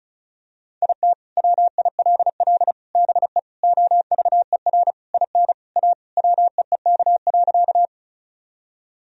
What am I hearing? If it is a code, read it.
Morse code, heard as a keyed tone, about 36 wpm, 700 Hz. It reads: IT WILL BE OVER IN A WEEK.
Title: Sentences from Top 1000 Words - Review Part 2 - 35wpm